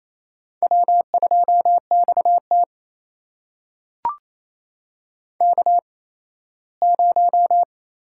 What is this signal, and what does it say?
Morse code, heard as a keyed tone, about 28 wpm, spaced as for 17 wpm, 700 Hz. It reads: W2XT K0
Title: USA Call Signs - 4 Characters - 8x Character Spacing - 28wpm